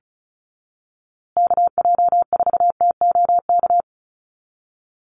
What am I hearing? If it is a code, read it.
Morse code, heard as a keyed tone, about 34 wpm, 700 Hz. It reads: KJ4TOK
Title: USA Call Signs Repeated 3x 35wpm